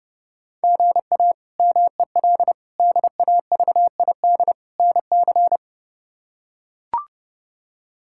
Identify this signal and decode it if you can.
Morse code, heard as a keyed tone, about 31 wpm, 700 Hz. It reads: GA MEL DAVID NC
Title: K1USN SST QSO Practice - 30wpm